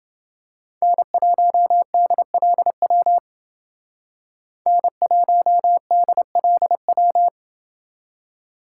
Morse code, heard as a keyed tone, about 29 wpm, 700 Hz. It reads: N1DLW N1DLW